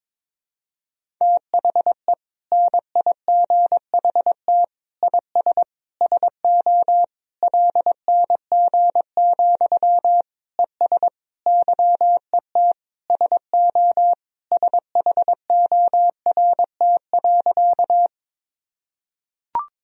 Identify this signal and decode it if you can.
Morse code, heard as a keyed tone, about 22 wpm, 700 Hz. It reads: THE NIGHT IS SO LNG, ES YET SO SHORT.
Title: Sentences from Top 500 Words and Abbreviations 22wpm